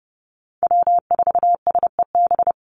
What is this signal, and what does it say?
Morse code, heard as a keyed tone, about 30 wpm, 700 Hz. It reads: W4SEB
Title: USA Call Signs 30wpm